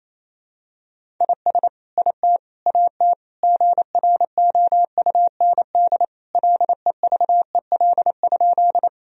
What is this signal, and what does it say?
Morse code, heard as a keyed tone, about 28 wpm, 700 Hz. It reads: IS IT AT GROUND LEVEL?
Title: Sentences from Top 1000 Words - Review Part 4 - 28wpm